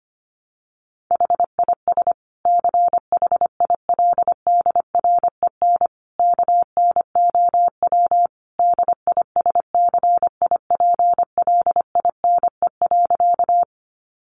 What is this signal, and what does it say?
Morse code, heard as a keyed tone, about 25 wpm, 700 Hz. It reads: HIS CHILDREN KNOW DISCIPLINE.
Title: Sentences from Top 900 Words 25wpm